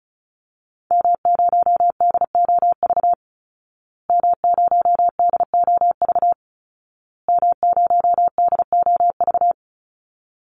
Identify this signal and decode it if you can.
Morse code, heard as a keyed tone, about 34 wpm, 700 Hz. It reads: M0DOV M0DOV M0DOV
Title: Call Signs Repeated 2x - 35wpm